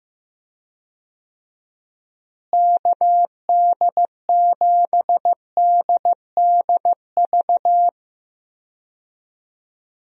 Morse code, heard as a keyed tone, about 15 wpm, 700 Hz. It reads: KD7DDV
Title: USA Call Signs - 6 Characters - Repeated 2x - 15wpm